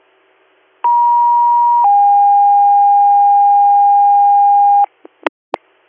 Two-tone page, heard as a one-second tone, then a three-second tone, about 950 then 810 Hz. It is heard from 0.8 to 4.8 s.